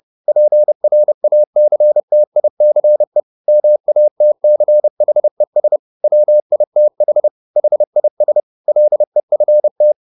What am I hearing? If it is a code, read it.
Morse code, heard as a keyed tone, about 30 wpm, 600 Hz. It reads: PRACTICE MATCHES WITH HIS LEFT